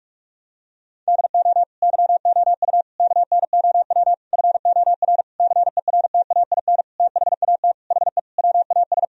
Morse code, heard as a keyed tone, about 46 wpm, 700 Hz. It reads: DO YOU KNOW FOR CERTAIN THAT HE WAS
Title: Sentences from Top 1000 Words - Review Part 2 - 45wpm